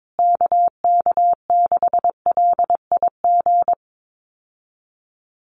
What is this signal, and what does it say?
Morse code, heard as a keyed tone, about 22 wpm, 700 Hz. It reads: KK6LIG
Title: USA Call Signs Repeated 3x 22wpm